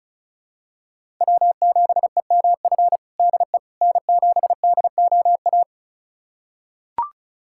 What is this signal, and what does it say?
Morse code, heard as a keyed tone, about 36 wpm, 700 Hz. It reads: W7EMF DE N7DOA
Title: USA Callsign DE Callsign - 5 Characters - 35wpm